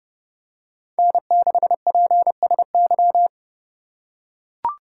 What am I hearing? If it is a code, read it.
Morse code, heard as a keyed tone, about 30 wpm, 700 Hz. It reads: N6PSY